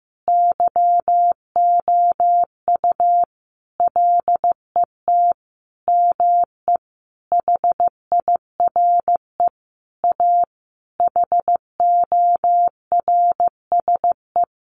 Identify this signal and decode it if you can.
Morse code, heard as a keyed tone, about 15 wpm, 700 Hz. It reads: YOU LET ME HIRE A HORSE
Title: Sentences from Top 1400 Words 15wpm